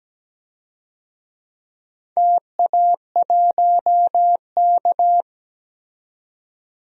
Morse code, heard as a keyed tone, about 17 wpm, 700 Hz. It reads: TA1K
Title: DXCC-100 Call Signs - 4 Characters - 17wpm